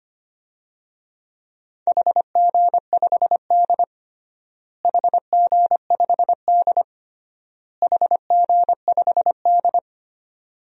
Morse code, heard as a keyed tone, about 25 wpm, 700 Hz. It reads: HG5D HG5D HG5D
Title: DXCC-100 Call Signs - 4 Characters - Repeated 3x - 25wpm